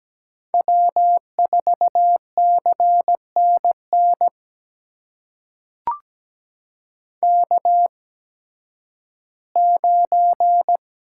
Morse code, heard as a keyed tone, about 17 wpm, spaced as for 11 wpm, 700 Hz. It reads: W4CNN K9
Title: USA Call Signs - 5 Characters - 8x Character Spacing - 17wpm